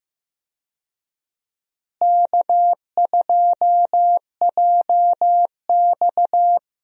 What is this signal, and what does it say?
Morse code, heard as a keyed tone, about 15 wpm, 700 Hz. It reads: K2JX
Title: USA Call Signs - 4 Characters - Repeated 2x - 15wpm